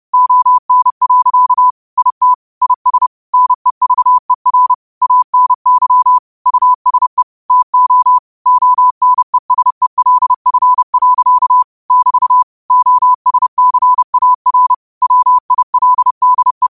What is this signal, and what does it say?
Morse code, heard as a keyed tone, about 30 wpm, 1000 Hz. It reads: ON. IT IS NEVER ANY USE TO ONESELF. <BT> OSCAR WILDE
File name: qotdfiles / 30wpm